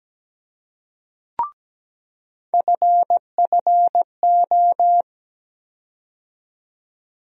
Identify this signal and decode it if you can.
Morse code, heard as a keyed tone, about 17 wpm, 700 Hz. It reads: FFO
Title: USA Call Sign Postfixes - 3 Letters 17wpm